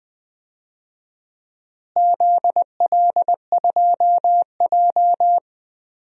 Morse code, heard as a keyed tone, about 20 wpm, 700 Hz. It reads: ZL2J
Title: DXCC-100 Call Signs - 4 Characters - Repeated 3x - 20wpm